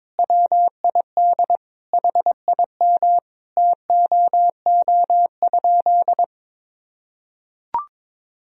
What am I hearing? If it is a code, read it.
Morse code, heard as a keyed tone, about 22 wpm, 700 Hz. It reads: WID HIM TOO?
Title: Sentences from Top 500 Words and Abbreviations 22wpm